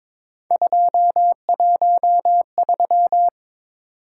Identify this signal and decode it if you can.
Morse code, heard as a keyed tone, about 26 wpm, 700 Hz. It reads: 213
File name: Speed-Racing - Three Letter-Number Combinations - 22wpm